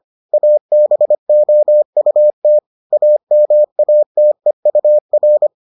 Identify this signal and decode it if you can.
Morse code, heard as a keyed tone, about 25 wpm, 600 Hz. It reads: ABOUT AMATEUR